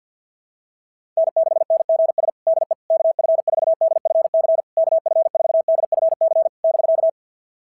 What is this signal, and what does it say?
Morse code, heard as a keyed tone, about 49 wpm, 660 Hz. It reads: N6NGS DE KF4DFC KF4DFC <BK>
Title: SOTA QSO Practice 50wpm